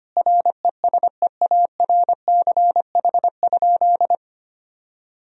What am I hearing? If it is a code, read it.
Morse code, heard as a keyed tone, about 25 wpm, 700 Hz. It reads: RESEARCH?
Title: Sentences from Top 1000 Words - Review Part 4 - 25wpm